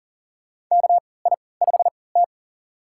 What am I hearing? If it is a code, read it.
Morse code, heard as a keyed tone, about 39 wpm, spaced as for 22 wpm, 700 Hz. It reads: KI5T
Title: USA Call Signs - 4 Characters - 3x Character Spacing - 40wpm